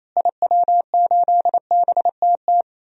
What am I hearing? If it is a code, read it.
Morse code, heard as a keyed tone, about 28 wpm, 700 Hz. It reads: IW8BTT
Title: DXCC-100 Call Signs - 6 Characters - Repeated 3x - 28wpm